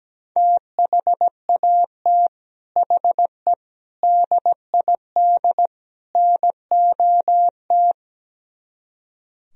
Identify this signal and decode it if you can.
Morse code, heard as a keyed tone, about 17 wpm, 700 Hz. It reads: THAT HE DID NOT
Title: Speed-Racing - Sets of 4 Words 17wpm